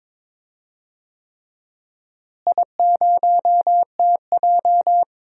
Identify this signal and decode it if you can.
Morse code, heard as a keyed tone, about 22 wpm, 700 Hz. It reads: I0TJ
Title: DXCC-100 Call Signs - 4 Characters - Repeated 3x - 22wpm